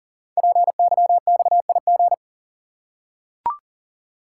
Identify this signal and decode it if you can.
Morse code, heard as a keyed tone, about 39 wpm, 700 Hz. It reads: PYXIG
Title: Code Groups - 5 Letters - 40wpm